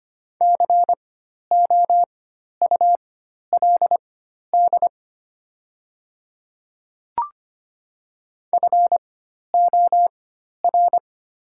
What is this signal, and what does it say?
Morse code, heard as a keyed tone, about 25 wpm, spaced as for 12 wpm, 700 Hz. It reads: COULD FOR
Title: 4x Character Spacing - Top 200 Words - 25wpm